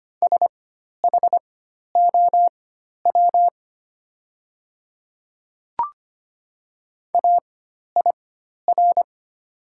Morse code, heard as a keyed tone, about 25 wpm, spaced as for 12 wpm, 690 Hz. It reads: SHOW AIR
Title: 4x Character Spacing - Top 200 Words - 25wpm